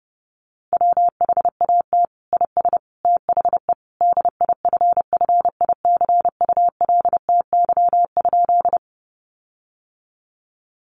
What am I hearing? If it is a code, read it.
Morse code, heard as a keyed tone, about 30 wpm, 700 Hz. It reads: WHAT IS THE DIFFICULTY?
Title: Sentences from Top 1500 Words 30wpm